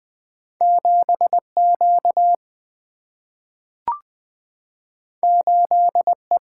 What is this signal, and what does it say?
Morse code, heard as a keyed tone, about 20 wpm, 700 Hz. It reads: 7 Q 8 E
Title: Single-Letter Single-Number 20wpm